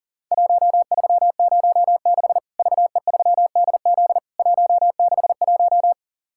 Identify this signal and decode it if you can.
Morse code, heard as a keyed tone, about 40 wpm, 700 Hz. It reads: 1306 VE3DZ 161
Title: Worked All Europe DX Contest - QTC Exchange - 40wpm